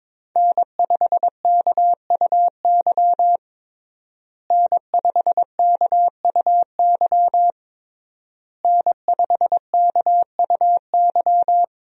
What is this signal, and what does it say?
Morse code, heard as a keyed tone, about 21 wpm, 700 Hz. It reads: N5KUY N5KUY N5KUY
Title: USA Call Signs - 5 Characters - Repeated 3x - 22wpm